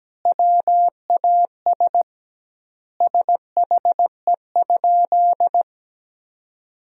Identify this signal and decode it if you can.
Morse code, heard as a keyed tone, about 17 wpm, 700 Hz. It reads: WAS SHE?
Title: 2x Word-Spacing - Sentences from Top 100 Words - 17wpm